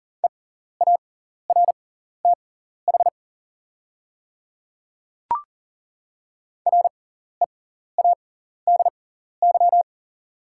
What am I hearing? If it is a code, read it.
Morse code, heard as a keyed tone, about 40 wpm, spaced as for 14 wpm, 690 Hz. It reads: EARTH READY
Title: 6x Character Spacing - Top 500 Words - 40wpm